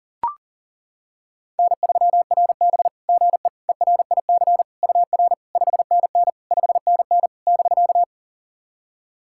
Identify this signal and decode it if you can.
Morse code, heard as a keyed tone, about 40 wpm, 700 Hz. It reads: N3RB GE ERIC UR 5NN 5NN <BK>
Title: SOTA QSO Practice 40wpm